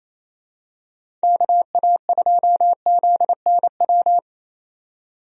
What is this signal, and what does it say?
Morse code, heard as a keyed tone, about 28 wpm, 700 Hz. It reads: KA2ZNW
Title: USA Call Signs - 6 Characters - 28wpm